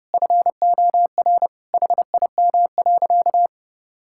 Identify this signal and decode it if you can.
Morse code, heard as a keyed tone, about 30 wpm, 700 Hz. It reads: FOR HIM.
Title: Sentences from Top 1000 Words - Review Part 5 - 30wpm